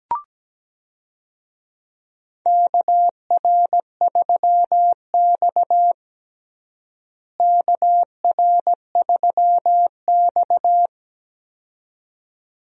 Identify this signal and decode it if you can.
Morse code, heard as a keyed tone, about 17 wpm, 700 Hz. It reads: KR3X KR3X
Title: USA Call Signs - 4 Characters - Repeated 2x - 17wpm